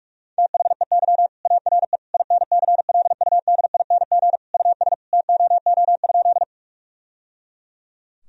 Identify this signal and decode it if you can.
Morse code, heard as a keyed tone, about 46 wpm, 700 Hz. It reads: THEY ARE INCLUDING US TOO?
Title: Sentences from Top 1000 Words - Review Part 4 - 45wpm